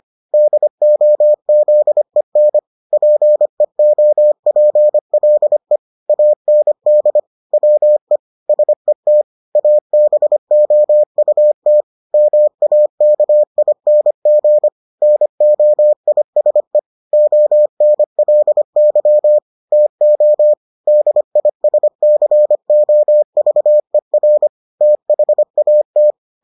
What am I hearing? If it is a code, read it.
Morse code, heard as a keyed tone, about 25 wpm, 600 Hz. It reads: DOZEN PEOPLE AND WE SET ABOUT MAKING NOISE ONLY TO DISCOVER THAT